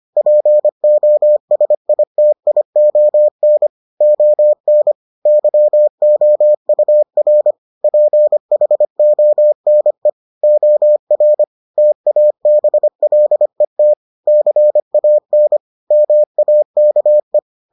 Morse code, heard as a keyed tone, about 25 wpm, 600 Hz. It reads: POSITION ON YOUR PHONE OR TABLET CAN MAKE